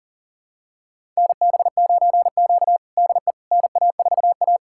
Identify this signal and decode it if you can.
Morse code, heard as a keyed tone, about 39 wpm, 700 Hz. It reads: NB9Q DE NA4A
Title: USA Callsign DE Callsign - 4 Characters - 40wpm